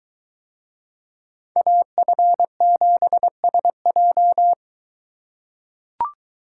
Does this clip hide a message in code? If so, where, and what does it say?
Yes: Morse code from 1.6 to 4.5 s, heard as a keyed tone, about 23 wpm, 700 Hz. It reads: AF7SJ